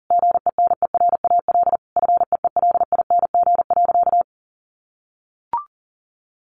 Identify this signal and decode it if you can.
Morse code, heard as a keyed tone, about 40 wpm, 700 Hz. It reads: GENERAL FEELING.
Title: Sentences from Top 1400 Words 40wpm